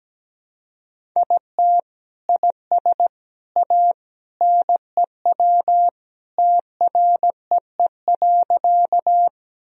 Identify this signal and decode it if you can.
Morse code, heard as a keyed tone, about 17 wpm, 700 Hz. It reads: IT IS A NEW TREE.
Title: Sentences from Top 1000 Words - Review Part 1 - 17wpm